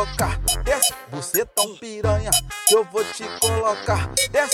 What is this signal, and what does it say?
There is no code.